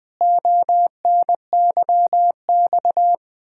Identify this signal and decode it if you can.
Morse code, heard as a keyed tone, about 20 wpm, 700 Hz. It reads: ONYX
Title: Pangram Challenge v2 - 20wpm